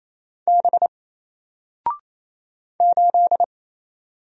Morse code, heard as a keyed tone, about 27 wpm, 700 Hz. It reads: B 8
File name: Single-Letter-Number-28wpm